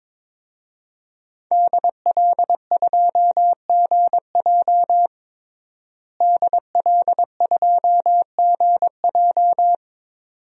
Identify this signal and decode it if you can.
Morse code, heard as a keyed tone, about 22 wpm, 700 Hz. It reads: DL2GJ DL2GJ
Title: DXCC-100 Call Signs - 5 Characters - Repeated 2x - 22wpm